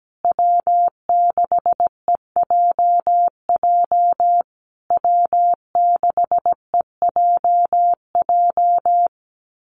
Morse code, heard as a keyed tone, about 17 wpm, 700 Hz. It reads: W6EJJ W6EJJ
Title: USA Call Signs Repeated 3x 17wpm